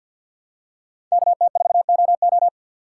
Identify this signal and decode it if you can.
Morse code, heard as a keyed tone, about 53 wpm, 700 Hz. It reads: KT4OO
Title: USA Call Signs - 4x Character Spacing - 50wpm